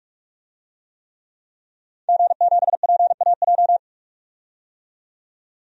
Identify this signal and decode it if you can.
Morse code, heard as a keyed tone, about 43 wpm, 700 Hz. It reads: G7PAJ
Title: DXCC-100 Call Signs - 5 Characters - 45wpm